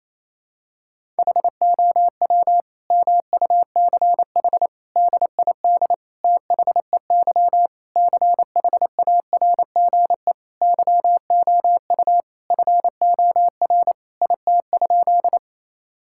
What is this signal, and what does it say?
Morse code, heard as a keyed tone, about 28 wpm, 700 Hz. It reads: HOW MUCH DID THEY CHARGE YOU FOR IT?
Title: Sentences from Top 1000 Words - Review Part 6 - 28wpm